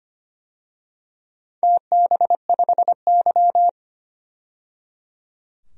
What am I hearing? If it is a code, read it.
Morse code, heard as a keyed tone, about 25 wpm, 690 Hz. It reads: TB5Y
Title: DXCC-100 Call Signs - 4 Characters - 25wpm